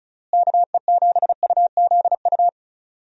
Morse code, heard as a keyed tone, about 34 wpm, 700 Hz. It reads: KE7UZU
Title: USA Call Signs - 6 Characters - Repeated 3x - 35wpm